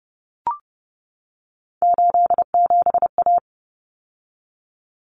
Morse code, heard as a keyed tone, about 30 wpm, 700 Hz. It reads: 87A
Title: Common Cut Numbers - 3-Digit 30wpm